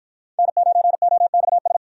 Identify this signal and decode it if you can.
Morse code, heard as a keyed tone, about 49 wpm, 700 Hz. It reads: N9OXS